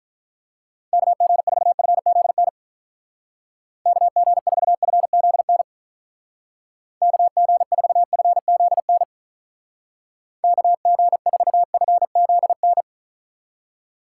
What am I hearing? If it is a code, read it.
Morse code, heard as a keyed tone, about 45 wpm, 700 Hz. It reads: KG4FZN KG4FZN KG4FZN KG4FZN